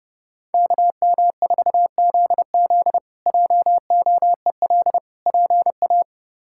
Morse code, heard as a keyed tone, about 32 wpm, 700 Hz. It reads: KM4ZZ JOEL PA
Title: K1USN SST QSO Practice - 30wpm